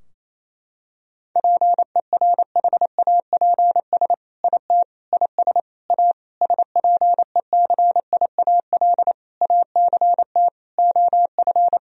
Morse code, heard as a keyed tone, about 28 wpm, 700 Hz. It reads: PERHAPS IT IS A SPECIAL ACT OF
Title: Sentences from Top 1000 Words - Review Part 3 - 28wpm